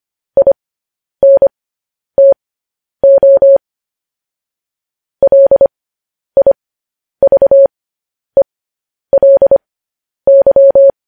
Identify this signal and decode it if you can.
Morse code, heard as a keyed tone, about 25 wpm, spaced as for 10 wpm, 550 Hz. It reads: INTO LIVELY
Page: News Headlines in Morse Code at 10 WPM